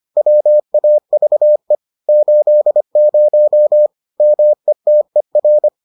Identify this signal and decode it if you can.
Morse code, heard as a keyed tone, about 25 wpm, 600 Hz. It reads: WAVE 80 METER